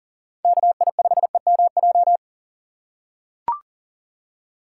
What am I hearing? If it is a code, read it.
Morse code, heard as a keyed tone, about 40 wpm, 700 Hz. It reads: KI5EMJ
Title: USA Call Signs - 6 Characters - 4x Character Spacing - 40wpm